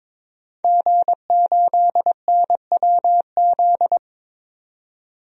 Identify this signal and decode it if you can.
Morse code, heard as a keyed tone, about 22 wpm, 700 Hz. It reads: G8NWZ
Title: DXCC-100 Call Signs - 5 Characters - 22wpm